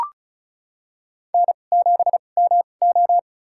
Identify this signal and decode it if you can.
Morse code, heard as a keyed tone, about 36 wpm, spaced as for 25 wpm, 700 Hz. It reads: N7MO
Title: USA Call Signs - 2x Character Spacing - 35wpm